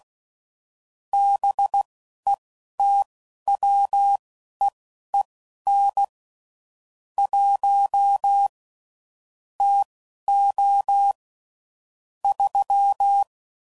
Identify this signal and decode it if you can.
Morse code, heard as a keyed tone, about 16 wpm, spaced as for 11 wpm, 800 Hz. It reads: BETWEEN 1 TO 3
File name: CW Code Groups @10WPM / Weather Sets / Weather 5 5